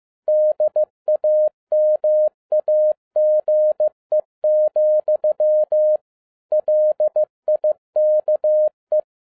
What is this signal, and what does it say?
Morse code, heard as a keyed tone, about 15 wpm, 610 Hz. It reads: DAMAGE, LIKE
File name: 2011-10-mp3 / n02-FIELD DAY CONTEST QRV FROM SAKAE VILLAGE 15wpm